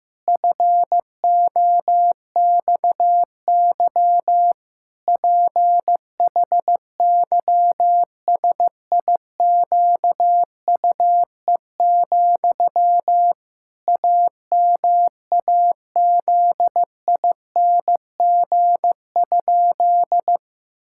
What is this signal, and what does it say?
Morse code, heard as a keyed tone, about 15 wpm, 700 Hz. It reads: FOXY PHYSIQUE, AMAZING?
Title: Pangram Challenge v2 - 15wpm